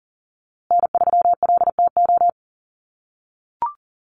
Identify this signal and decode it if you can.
Morse code, heard as a keyed tone, about 40 wpm, 700 Hz. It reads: N3LTO
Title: USA Call Signs Repeated 3x 40wpm